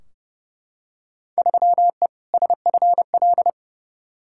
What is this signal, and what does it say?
Morse code, heard as a keyed tone, about 30 wpm, 700 Hz. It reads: 3E SFL
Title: Field Day Exchanges Encore - 30wpm